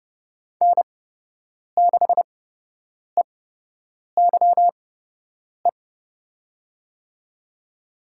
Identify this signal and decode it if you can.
Morse code, heard as a keyed tone, about 30 wpm, spaced as for 8 wpm, 700 Hz. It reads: N6EYE
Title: USA Call Signs - 5 Characters - 8x Character Spacing - 30wpm